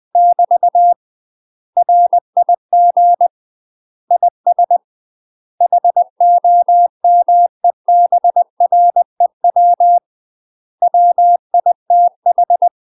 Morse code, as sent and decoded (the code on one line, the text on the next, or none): -...- / .-. .. --. / .. ... / .... --- -- . -... .-. .-- / .-- .. - ....
<BT> RIG IS HOMEBREW WITH